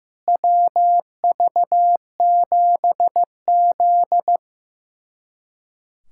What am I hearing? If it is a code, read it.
Morse code, heard as a keyed tone, about 15 wpm, 700 Hz. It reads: WV7Z
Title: USA Callsign DE Callsign - 4 Characters - 15wpm